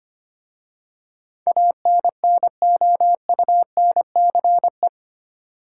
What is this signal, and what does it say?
Morse code, heard as a keyed tone, about 25 wpm, 690 Hz. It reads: ANNOUNCE
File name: 8 Letter Words - First 3 Letters Given - 25wpm